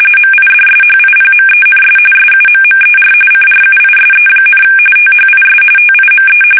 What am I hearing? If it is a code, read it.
RTTY, about 45.45 baud. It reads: MIDSHIPMENS SUMMER CRUISE, THIS TIME
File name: fsk850 / battleshipnj